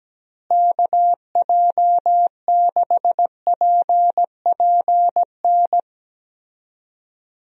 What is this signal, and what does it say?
Morse code, heard as a keyed tone, about 17 wpm, 700 Hz. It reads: KJ6PPN
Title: USA Call Signs - 6 Characters - Repeated 2x - 17wpm